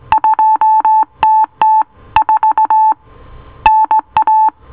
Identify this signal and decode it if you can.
Morse code, heard as a keyed tone, about 18 wpm, 870 Hz. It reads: 2TT4 NA